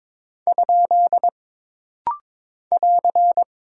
Morse code, heard as a keyed tone, about 22 wpm, 700 Hz. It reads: ? <AR>
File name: practice / 22wpm / Single Punctuation-Prosign - Rapid-Fire - 22wpm